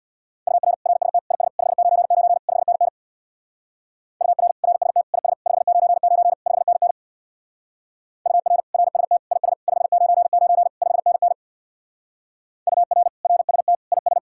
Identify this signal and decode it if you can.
Morse code, heard as a keyed tone, about 59 wpm, 700 Hz. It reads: UR RST IS 599 5NN UR RST IS 599 5NN UR RST IS 599 5NN UR RST IS
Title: Speed-Racing Common QSO Phrases Encore - 50wpm